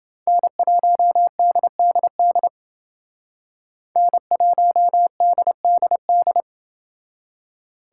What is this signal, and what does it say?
Morse code, heard as a keyed tone, about 28 wpm, 700 Hz. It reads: N1DDD N1DDD